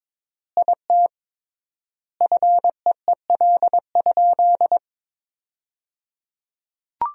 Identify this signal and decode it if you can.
Morse code, heard as a keyed tone, about 22 wpm, 700 Hz. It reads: IT FEEL?
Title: 3x Word-Spacing - Sentences from Top 400 Words - 22wpm